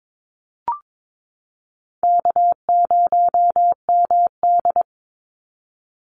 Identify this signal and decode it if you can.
Morse code, heard as a keyed tone, about 22 wpm, 690 Hz. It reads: K0MD
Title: Call Signs Repeated 2x - 22wpm